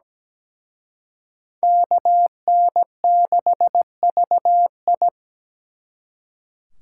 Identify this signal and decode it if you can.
Morse code, heard as a keyed tone, about 20 wpm, 700 Hz. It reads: KN6VI
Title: Speed Racing - USA 2x2 Call Signs 17wpm